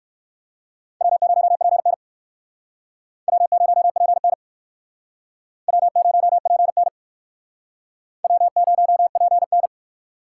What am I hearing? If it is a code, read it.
Morse code, heard as a keyed tone, about 68 wpm, 700 Hz. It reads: W0PN W0PN W0PN W0PN